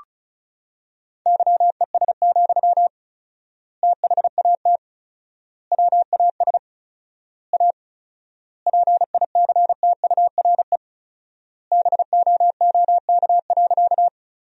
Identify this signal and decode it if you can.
Morse code, heard as a keyed tone, about 34 wpm, 700 Hz. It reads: YES, THAT WAS A PICTURE BOOK.
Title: 4x Word-Spacing - Sentences from Top 500 Words - 35wpm